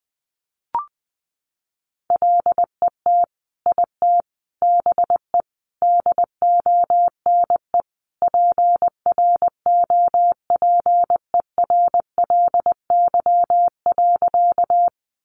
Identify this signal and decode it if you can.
Morse code, heard as a keyed tone, about 20 wpm, 700 Hz. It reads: LET IT BE DONE PROPERLY.
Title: Sentences from Top 1100 Words 20wpm